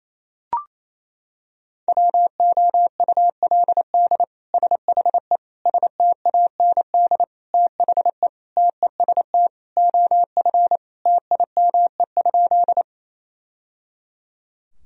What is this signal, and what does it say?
Morse code, heard as a keyed tone, about 27 wpm, 700 Hz. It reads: WOULD SHE STAND THE TEST OF TIME?
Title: Sentences from Top 1000 Words - Review Part 3 - 28wpm